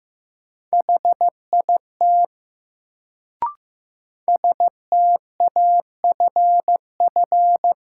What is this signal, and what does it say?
Morse code, heard as a keyed tone, about 15 wpm, 700 Hz. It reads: HIT STAFF
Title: Sentences from Top 1200 Words - New Word Drill - 15wpm